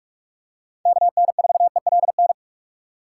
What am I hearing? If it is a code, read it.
Morse code, heard as a keyed tone, about 42 wpm, 700 Hz. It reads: KN4ELN